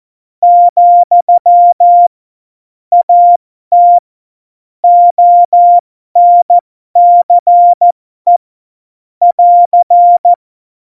Morse code, heard as a keyed tone, about 14 wpm, spaced as for 12 wpm, 700 Hz. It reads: , AT ONCE <AR>
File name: Morsekurs / Witze